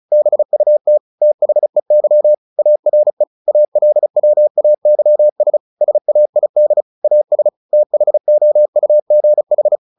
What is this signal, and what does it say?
Morse code, heard as a keyed tone, about 35 wpm, 600 Hz. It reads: BUT THEY ARE ALWAYS SAID AS THOUGH
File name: the old man and the sea / the old man and the sea 1 35wpm